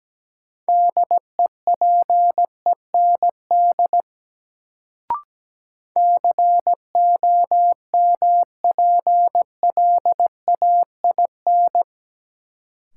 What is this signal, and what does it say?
Morse code, heard as a keyed tone, about 17 wpm, 700 Hz. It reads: DEPEND COMPLAIN
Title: Sentences from Top 1300 Words - New Word Drill - 17wpm